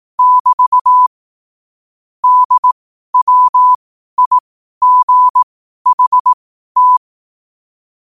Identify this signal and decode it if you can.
Morse code, heard as a keyed tone, about 18 wpm, spaced as for 12 wpm, 1000 Hz. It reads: <BT> DWIGHT
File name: qotdfiles / 12wpm